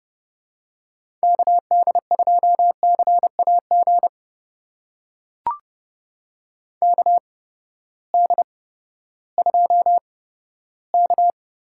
Morse code, heard as a keyed tone, about 32 wpm, spaced as for 15 wpm, 700 Hz. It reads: KD2CAG KD2K
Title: USA Call Signs - 6 Characters - 8x Character Spacing - 30wpm